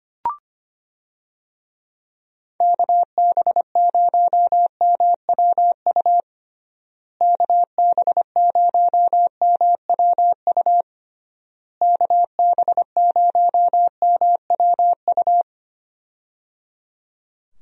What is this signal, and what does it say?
Morse code, heard as a keyed tone, about 25 wpm, 700 Hz. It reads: KB0MWU KB0MWU KB0MWU
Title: USA Call Signs - 6 Characters - Repeated 3x - 25wpm